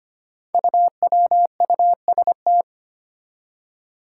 Morse code, heard as a keyed tone, about 25 wpm, 700 Hz. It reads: UWUST
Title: Five Challenging Letter-Number Combinations 25wpm